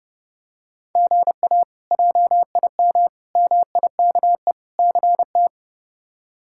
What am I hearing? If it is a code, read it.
Morse code, heard as a keyed tone, about 32 wpm, 700 Hz. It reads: GA JIM MIKE CT
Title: K1USN SST QSO Practice - 30wpm